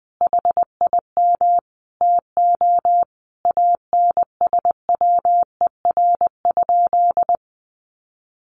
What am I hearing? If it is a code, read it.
Morse code, heard as a keyed tone, about 20 wpm, 700 Hz. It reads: HIM TO ANSWER?
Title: Sentences from Top 1100 Words 20wpm